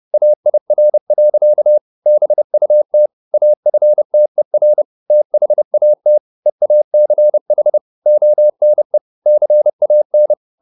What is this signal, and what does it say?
Morse code, heard as a keyed tone, about 30 wpm, 600 Hz. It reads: AIR. BUT AFTER THAT EACH ONE CAN